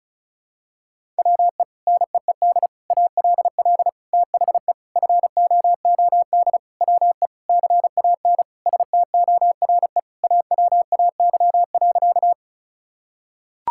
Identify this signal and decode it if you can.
Morse code, heard as a keyed tone, about 36 wpm, 700 Hz. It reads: WE NEED ALL THE FOOD WE CAN STORE AWAY.
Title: Sentences from Top 1000 Words - Review Part 5 - 35wpm